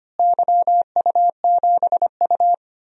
Morse code, heard as a keyed tone, about 25 wpm, 700 Hz. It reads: YU7U
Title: DXCC-100 Call Signs - 4 Characters - Repeated 3x - 25wpm